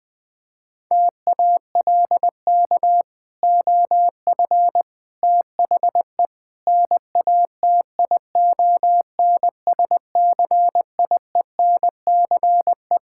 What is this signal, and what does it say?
Morse code, heard as a keyed tone, about 20 wpm, 700 Hz. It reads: TALK OF THE NATION SCIENCE
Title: Build A Phrase - 5-Word Phrases - 20wpm